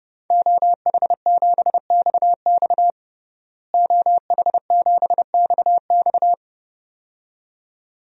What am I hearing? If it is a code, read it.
Morse code, heard as a keyed tone, about 30 wpm, 700 Hz. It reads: OH7XX OH7XX
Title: DXCC-100 Call Signs - 5 Characters - Repeated 2x - 30wpm